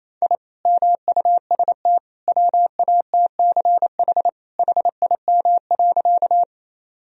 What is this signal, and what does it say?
Morse code, heard as a keyed tone, about 28 wpm, 700 Hz. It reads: I MUST WATCH HIM.
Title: Sentences from Top 1000 Words - Review Part 1 - 28wpm